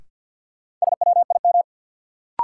Morse code, heard as a keyed tone, about 50 wpm, 700 Hz. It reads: SWIM
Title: Sentences from Top 1000 Words - Review Part 6 - New Word Drill - 50wpm